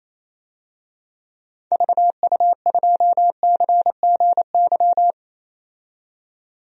Morse code, heard as a keyed tone, about 27 wpm, 700 Hz. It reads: VU2CGY
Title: DXCC-100 Call Signs - 6 Characters - 28wpm